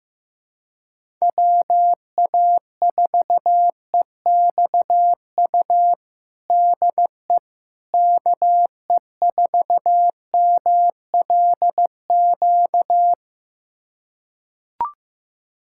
Morse code, heard as a keyed tone, about 15 wpm, 700 Hz. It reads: WA4EXU DE KE4MLQ
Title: USA Callsign DE Callsign - 6 Characters - 15wpm